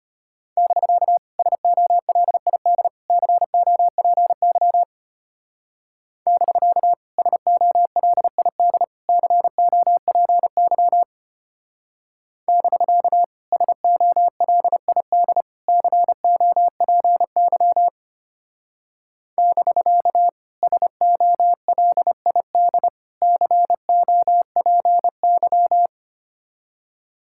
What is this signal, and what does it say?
Morse code, heard as a keyed tone, about 32 wpm, 700 Hz. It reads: <BK> SOLID COPY <BK> SOLID COPY <BK> SOLID COPY <BK> SOLID COPY